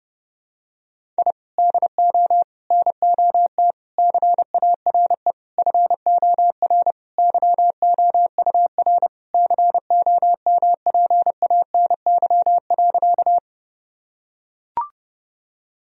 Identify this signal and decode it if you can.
Morse code, heard as a keyed tone, about 30 wpm, 700 Hz. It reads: I DO NOT CARE FOR YOUR COMPANY.
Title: Sentences from Top 1000 Words - Review Part 3 - 30wpm